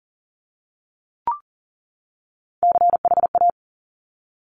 Morse code, heard as a keyed tone, about 43 wpm, 700 Hz. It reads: CHA
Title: Top Three Letter Combinations in Words 40wpm